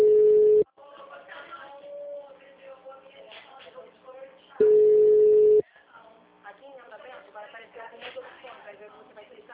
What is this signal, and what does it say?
Ringback tone (Europe, 425 Hz), with speech.
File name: out- -2002- - - .7585